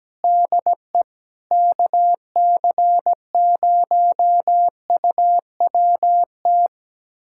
Morse code, heard as a keyed tone, about 17 wpm, 700 Hz. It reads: DE KC0UWT
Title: USA Callsign DE Callsign - 6 Characters - 17wpm